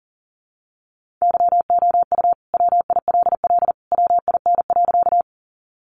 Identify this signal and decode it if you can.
Morse code, heard as a keyed tone, about 40 wpm, 700 Hz. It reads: YOU WILL WIN.
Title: Sentences from Top 1500 Words 40wpm